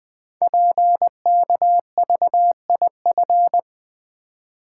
Morse code, heard as a keyed tone, about 20 wpm, 700 Hz. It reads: PKVIF
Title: Code Groups - 5 Letters - 20wpm